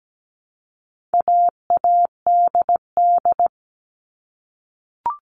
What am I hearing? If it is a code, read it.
Morse code, heard as a keyed tone, about 17 wpm, 700 Hz. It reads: AADD
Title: All Cut Numbers - 4-Digit 17wpm